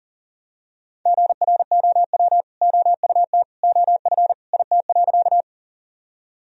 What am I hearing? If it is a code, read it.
Morse code, heard as a keyed tone, about 40 wpm, 700 Hz. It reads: GROW OUT OF IT.
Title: Sentences from Top 1000 Words - Review Part 1 - 40wpm